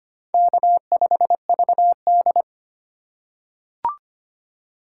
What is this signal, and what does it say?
Morse code, heard as a keyed tone, about 25 wpm, 700 Hz. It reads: K5VD
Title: USA Call Signs - 3x Character Spacing - 25wpm